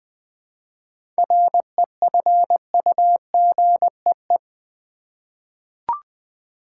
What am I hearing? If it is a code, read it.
Morse code, heard as a keyed tone, about 20 wpm, 700 Hz. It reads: REFUGEE
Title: 7 Letter Words - First 3 Letters Given - 20wpm